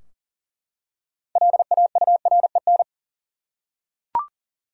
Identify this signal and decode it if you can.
Morse code, heard as a keyed tone, about 40 wpm, 700 Hz. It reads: LAUREN
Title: USA Common Names - 40wpm